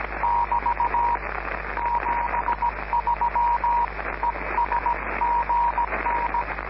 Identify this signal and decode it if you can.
Morse code, heard as a keyed tone, about 17 wpm, 950 Hz, with noise. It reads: <BT> 73 ES GN